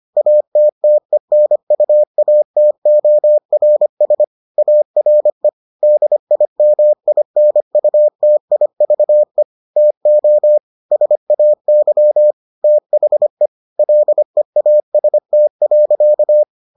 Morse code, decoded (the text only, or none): ATTENUATORS ARE DIMINUTIVE TO SAY THE LEAST.